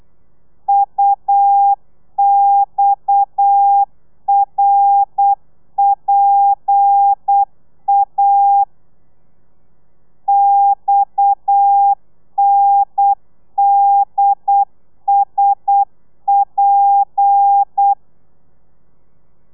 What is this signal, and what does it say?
Morse code, heard as a keyed tone, about 8 wpm, 800 Hz. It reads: UXRPA XNDSP